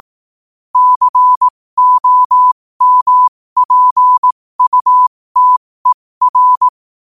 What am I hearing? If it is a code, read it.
Morse code, heard as a keyed tone, about 18 wpm, spaced as for 15 wpm, 1000 Hz. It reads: COMPUTER
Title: 15 WPM morse code quotes for Fri, 01 Mar 2024